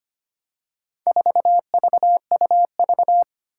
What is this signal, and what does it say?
Morse code, heard as a keyed tone, about 25 wpm, 700 Hz. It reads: 4VUV
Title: Four Challenging Letter-Number Combinations 25wpm